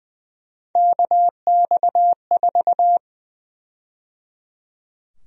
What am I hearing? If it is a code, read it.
Morse code, heard as a keyed tone, about 21 wpm, 700 Hz. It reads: KX4
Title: USA Call Sign Prefixes - 20wpm